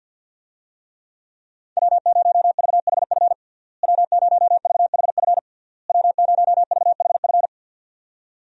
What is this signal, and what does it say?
Morse code, heard as a keyed tone, about 49 wpm, 700 Hz. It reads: W0VHF W0VHF W0VHF
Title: USA Call Signs - 5 Characters - Repeated 3x - 50wpm